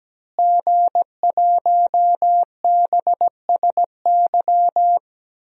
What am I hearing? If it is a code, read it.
Morse code, heard as a keyed tone, about 17 wpm, 700 Hz. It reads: G1BSY
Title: DXCC-100 Call Signs - 5 Characters - Repeated 3x - 17wpm